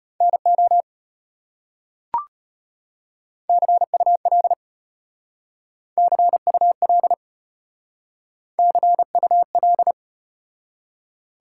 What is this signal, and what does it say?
Morse code, heard as a keyed tone, about 32 wpm, 700 Hz. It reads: NO CUL CUL CUL